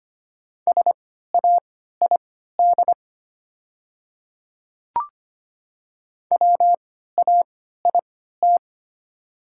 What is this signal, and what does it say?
Morse code, heard as a keyed tone, about 25 wpm, spaced as for 14 wpm, 700 Hz. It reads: SAID WAIT
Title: 3x Character Spacing - Top 500 Words - 25wpm